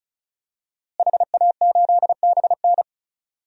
Morse code, heard as a keyed tone, about 36 wpm, 700 Hz. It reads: HA8BN
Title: DXCC-100 Call Signs - 5 Characters - Repeated 2x - 35wpm